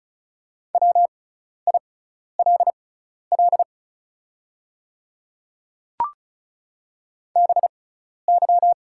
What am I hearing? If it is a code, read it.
Morse code, heard as a keyed tone, about 36 wpm, spaced as for 12 wpm, 700 Hz. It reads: WILL BY
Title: 6x Character Spacing - Top 200 Words - 35wpm